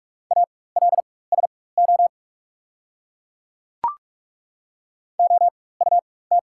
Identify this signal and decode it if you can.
Morse code, heard as a keyed tone, about 42 wpm, spaced as for 20 wpm, 690 Hz. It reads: ALSO OUT
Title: 4x Character Spacing - Top 200 Words - 45wpm